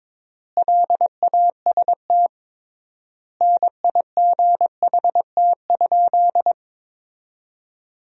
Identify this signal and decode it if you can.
Morse code, heard as a keyed tone, about 21 wpm, 700 Hz. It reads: LAST NIGHT?
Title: 3x Word-Spacing - Sentences from Top 300 Words - 22wpm